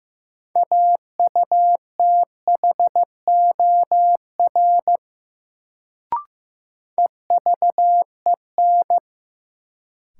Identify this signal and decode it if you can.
Morse code, heard as a keyed tone, about 15 wpm, 700 Hz. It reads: AUTHOR EVEN
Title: Sets of 4 Words - New Word Drill - 15wpm